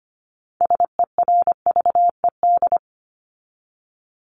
Morse code, heard as a keyed tone, about 25 wpm, 700 Hz. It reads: SERVED
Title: ED Suffix - Top 100 Words with Base Words 25wpm